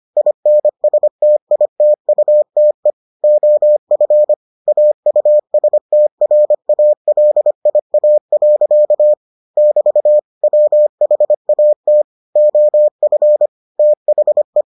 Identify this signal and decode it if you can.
Morse code, heard as a keyed tone, about 25 wpm, 600 Hz. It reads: INSTITUTE OF AUSTRALIA. <BT> WHAT OF THE